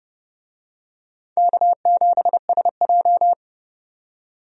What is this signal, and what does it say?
Morse code, heard as a keyed tone, about 32 wpm, 700 Hz. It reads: K7SJ
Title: USA Call Signs - 4 Characters - 30wpm